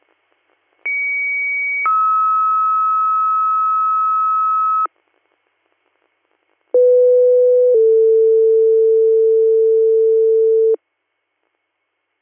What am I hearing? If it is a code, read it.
Two-tone page, heard as a one-second tone, then a three-second tone, about 2250 then 1290 Hz, about 500 then 450 Hz. It is heard from 0.9 to 10.7 s.